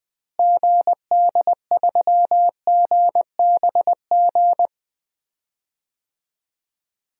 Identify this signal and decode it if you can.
Morse code, heard as a keyed tone, about 20 wpm, 700 Hz. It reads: GD3GBG
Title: DXCC-100 Call Signs - 6 Characters - Repeated 3x - 20wpm